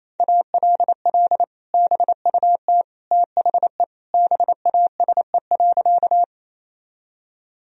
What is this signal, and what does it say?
Morse code, heard as a keyed tone, about 28 wpm, 700 Hz. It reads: ALL BUT THE BASE.
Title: Sentences from Top 1000 Words - Review Part 1 - 28wpm